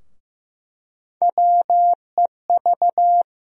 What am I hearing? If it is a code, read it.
Morse code, heard as a keyed tone, about 15 wpm, 700 Hz. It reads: WEV
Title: USA Call Sign Postfixes - 3 Letters 15wpm